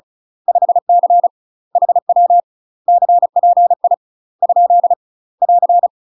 Morse code, heard as a keyed tone, about 35 wpm, 700 Hz. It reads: 5C HW CPI ? <AR>
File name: UK MASTER QSOs TRAINING / 35WPM / UK MASTER QSOs 0029